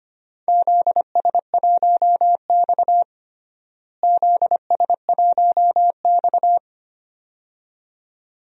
Morse code, heard as a keyed tone, about 25 wpm, 700 Hz. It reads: ZS1X ZS1X
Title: DXCC-100 Call Signs - 4 Characters - Repeated 2x - 25wpm